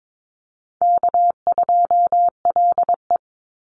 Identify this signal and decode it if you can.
Morse code, heard as a keyed tone, about 22 wpm, 700 Hz. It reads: K2LE
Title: Easy Call Signs Repeated 3x - 22wpm